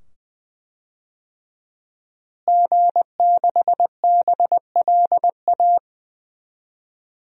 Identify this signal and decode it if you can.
Morse code, heard as a keyed tone, about 20 wpm, 700 Hz. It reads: G6BLA
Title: DXCC-100 Call Signs - 5 Characters - Repeated 2x - 20wpm